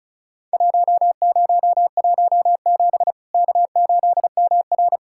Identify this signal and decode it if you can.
Morse code, heard as a keyed tone, about 36 wpm, 700 Hz. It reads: 1017 K8MR
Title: Worked All Europe DX Contest - QTC Exchange - 35wpm